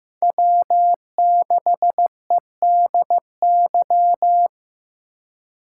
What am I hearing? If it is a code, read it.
Morse code, heard as a keyed tone, about 15 wpm, 700 Hz. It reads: W6EDY
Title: K1USN SST QSO Practice - 15wpm